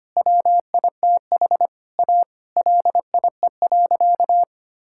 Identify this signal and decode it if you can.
Morse code, heard as a keyed tone, about 25 wpm, 700 Hz. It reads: WITH A LIE.
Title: Sentences from Top 1600 Words 25wpm